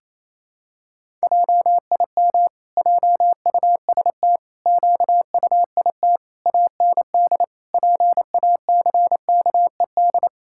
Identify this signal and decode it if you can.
Morse code, heard as a keyed tone, about 28 wpm, 700 Hz. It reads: JIM JUST QUIT AND PACKED
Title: Pangram Challenge v2 - 28wpm